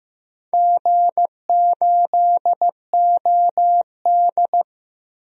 Morse code, heard as a keyed tone, about 15 wpm, 700 Hz. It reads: G8OD
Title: DXCC-100 Call Signs - 4 Characters - Repeated 3x - 15wpm